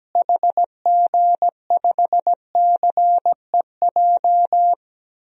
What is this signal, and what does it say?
Morse code, heard as a keyed tone, about 17 wpm, 700 Hz. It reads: HG5CEJ
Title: DXCC-100 Call Signs - 6 Characters - Repeated 3x - 17wpm